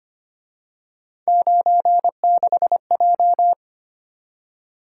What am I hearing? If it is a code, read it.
Morse code, heard as a keyed tone, about 29 wpm, 700 Hz. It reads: 96J
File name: Speed-Racing - Three Letter-Number Combinations - 25wpm